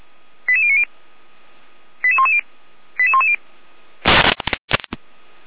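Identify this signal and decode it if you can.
Five-tone selective call (ZVEI), heard as quick runs of five tones, tones of about 70 ms. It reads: 88090 88110 88110
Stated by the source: ZVEI five-tone call 88012 received by SDR, from feuerwehr-orlamuende.org fire brigade audio server.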